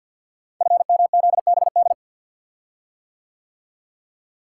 Morse code, heard as a keyed tone, about 52 wpm, 700 Hz. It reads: FMZBD
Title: Code Groups - 5 Letters - 50wpm